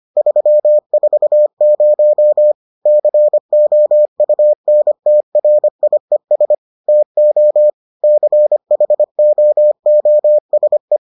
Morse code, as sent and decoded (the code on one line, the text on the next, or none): ...-- ....- ----- / -.-. --- ..- -. - .-. .. ... / - --- / -.-. .... --- --- ... .
340 COUNTRIES TO CHOOSE